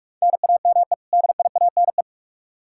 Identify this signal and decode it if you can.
Morse code, heard as a keyed tone, about 44 wpm, 700 Hz. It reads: NAME DIANE
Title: K1USN SST QSO Practice 45wpm